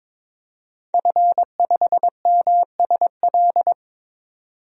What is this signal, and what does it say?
Morse code, heard as a keyed tone, about 22 wpm, 700 Hz. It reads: F5MSL
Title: DXCC-100 Call Signs - 5 Characters - 22wpm